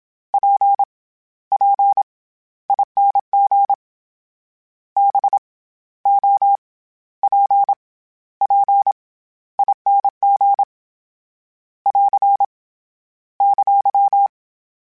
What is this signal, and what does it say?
Morse code, heard as a keyed tone, about 27 wpm, 800 Hz. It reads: P P ING B O P P ING <AR> !